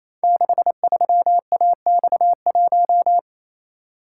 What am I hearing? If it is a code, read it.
Morse code, heard as a keyed tone, about 29 wpm, 700 Hz. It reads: 63AX1
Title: Code Groups - 5 Letter:Numbers - 28wpm